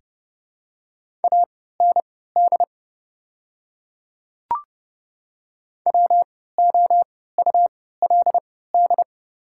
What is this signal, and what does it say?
Morse code, heard as a keyed tone, about 30 wpm, spaced as for 17 wpm, 700 Hz. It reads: AND WOULD
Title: 3x Character Spacing - Top 300 Words - 30wpm